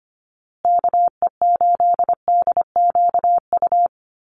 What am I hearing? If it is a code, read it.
Morse code, heard as a keyed tone, about 25 wpm, 700 Hz. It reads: KE8DQU
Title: USA Call Signs Repeated 3x 25wpm